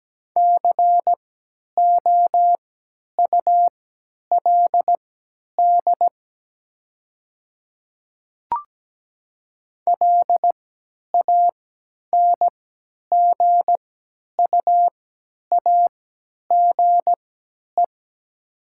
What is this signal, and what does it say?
Morse code, heard as a keyed tone, about 17 wpm, spaced as for 10 wpm, 700 Hz. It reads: COULD LANGUAGE
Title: 3x Character Spacing - Top 500 Words - 17wpm